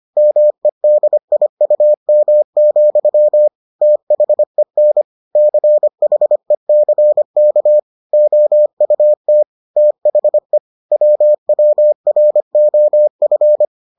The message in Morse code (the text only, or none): MEDIUM, THEN CHECK OUT THE WWROF